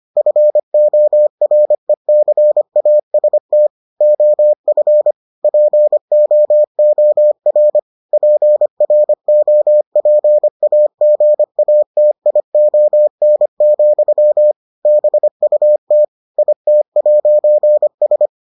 Morse code, heard as a keyed tone, about 25 wpm, 600 Hz. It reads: FORECAST OF POOR PROPAGATION, BUT IT'S